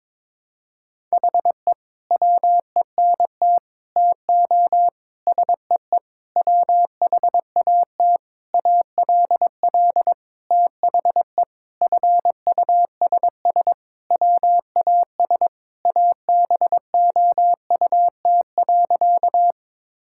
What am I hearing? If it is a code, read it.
Morse code, heard as a keyed tone, about 22 wpm, 700 Hz. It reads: HE WENT TO SEE WHAT ALL THE FUSS WAS ABOUT.
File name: practice / 22wpm / Sentences with 8-Word Phrases - 22wpm